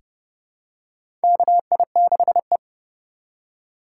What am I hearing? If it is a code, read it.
Morse code, heard as a keyed tone, about 30 wpm, 700 Hz. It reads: KI6E
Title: USA Call Signs - 4 Characters - 6x Character Spacing - 30wpm